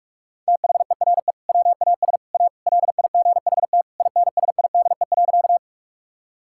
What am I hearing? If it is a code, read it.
Morse code, heard as a keyed tone, about 48 wpm, 700 Hz. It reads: THERE WAS A LIGHT INSIDE.
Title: Sentences from Top 1000 Words - Review Part 6 - 45wpm